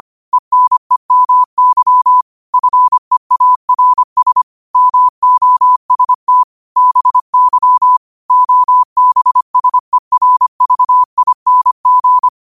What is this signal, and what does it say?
Morse code, heard as a keyed tone, about 25 wpm, 1000 Hz. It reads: ENEMY FEARS MOST BY OBSERVING
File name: qotdfiles / 25wpm